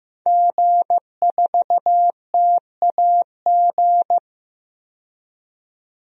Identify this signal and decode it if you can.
Morse code, heard as a keyed tone, about 15 wpm, 700 Hz. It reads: G4TAG
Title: DXCC-100 Call Signs - 5 Characters - 15wpm